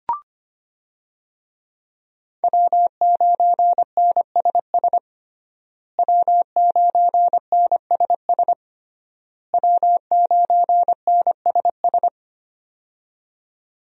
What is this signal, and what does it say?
Morse code, heard as a keyed tone, about 25 wpm, 700 Hz. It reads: W9NSS W9NSS W9NSS
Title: USA Call Signs - 5 Characters - Repeated 3x - 25wpm